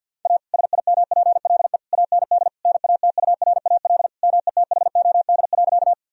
Morse code, heard as a keyed tone, about 50 wpm, 700 Hz. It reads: A SIMPLE AND NATURAL METHOD.
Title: Sentences from Top 1000 Words - Review Part 4 - 50wpm